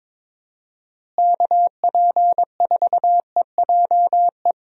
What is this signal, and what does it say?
Morse code, heard as a keyed tone, about 22 wpm, 700 Hz. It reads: KP4EJE
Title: DXCC-100 Call Signs - 6 Characters - Repeated 3x - 22wpm